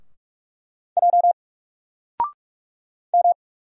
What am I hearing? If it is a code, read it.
Morse code, heard as a keyed tone, about 45 wpm, 700 Hz. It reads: J M
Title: Single Letter-Number 45wpm